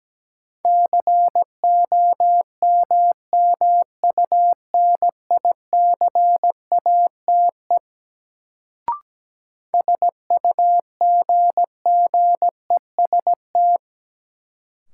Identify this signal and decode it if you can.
Morse code, heard as a keyed tone, about 17 wpm, 700 Hz. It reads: COMMUNICATE SUGGEST
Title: Sentences from Top 1300 Words - New Word Drill Alt - 17wpm